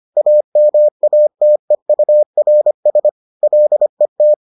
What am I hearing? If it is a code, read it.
Morse code, heard as a keyed tone, about 25 wpm, 600 Hz. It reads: AMATEURS LET